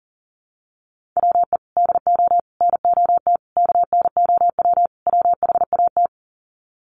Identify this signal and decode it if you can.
Morse code, heard as a keyed tone, about 40 wpm, 700 Hz. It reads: WE DO NOT KNOW WHAT
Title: Sets of 5 Words 40wpm